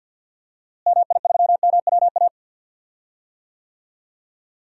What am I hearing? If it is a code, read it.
Morse code, heard as a keyed tone, about 46 wpm, 700 Hz. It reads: MI3MWA
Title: DXCC-100 Call Signs - 6 Characters - Repeated 3x - 50wpm